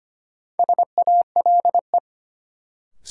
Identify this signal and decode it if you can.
Morse code, heard as a keyed tone, about 25 wpm, 700 Hz. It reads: SALE
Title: Sentences from Top 1400 Words - New Word Drill - 25wpm